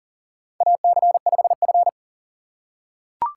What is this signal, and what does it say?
Morse code, heard as a keyed tone, about 40 wpm, 700 Hz. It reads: AC5F
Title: USA Call Signs - 4 Characters - 2x Character Spacing - 40wpm